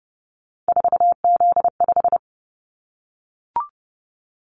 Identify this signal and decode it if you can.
Morse code, heard as a keyed tone, about 30 wpm, 700 Hz. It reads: 4Z5
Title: Callsign Prefixes for Top 50 DXCC - 30wpm